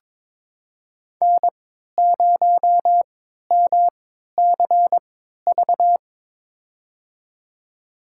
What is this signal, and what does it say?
Morse code, heard as a keyed tone, about 22 wpm, spaced as for 13 wpm, 700 Hz. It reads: N0MCV
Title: USA Call Signs - 5 Characters - 3x Character Spacing - 22wpm